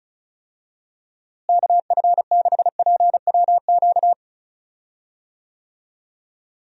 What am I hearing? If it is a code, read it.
Morse code, heard as a keyed tone, about 36 wpm, 700 Hz. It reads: KF6PWQ
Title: USA Call Signs - 6 Characters - Repeated 3x - 35wpm